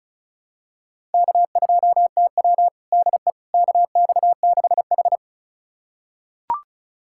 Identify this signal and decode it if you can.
Morse code, heard as a keyed tone, about 34 wpm, 700 Hz. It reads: K2TW DE KX6H
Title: USA Callsign DE Callsign - 4 Characters - 35wpm